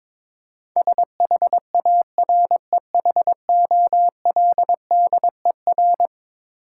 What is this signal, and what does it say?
Morse code, heard as a keyed tone, about 22 wpm, 700 Hz. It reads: SHAREHOLDER
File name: practice / 22wpm / Eleven Letter Words - 22wpm